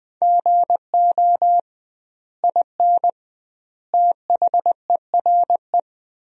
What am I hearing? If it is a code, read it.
Morse code, heard as a keyed tone, about 20 wpm, 700 Hz. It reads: GO IN THERE
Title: 2x Word-Spacing - Sentences from Top 500 Words - 20wpm